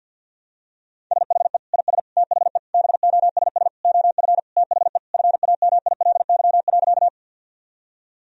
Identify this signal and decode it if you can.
Morse code, heard as a keyed tone, about 52 wpm, 700 Hz. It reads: SHE IS THE BOSS OF THE FAMILY.
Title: Sentences from Top 1000 Words - Review Part 4 - 50wpm